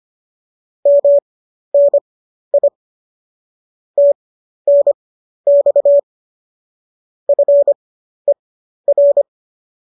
Morse code, heard as a keyed tone, about 25 wpm, spaced as for 12 wpm, 570 Hz. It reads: MNI TNX FER
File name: qso012 12